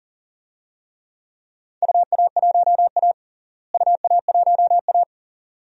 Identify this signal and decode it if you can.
Morse code, heard as a keyed tone, about 40 wpm, 700 Hz. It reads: UA1A UA1A
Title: DXCC-100 Call Signs - 4 Characters - Repeated 2x - 40wpm